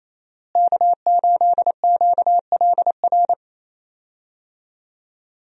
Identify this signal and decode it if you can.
Morse code, heard as a keyed tone, about 27 wpm, 700 Hz. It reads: K8QLR
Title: USA Call Signs - 5 Characters - Repeated 3x - 28wpm